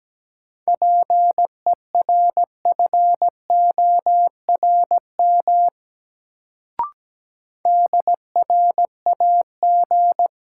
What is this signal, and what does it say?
Morse code, heard as a keyed tone, about 17 wpm, 700 Hz. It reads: PERFORM DRAG
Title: Sentences from Top 1500 Words - New Word Drill Alt - 17wpm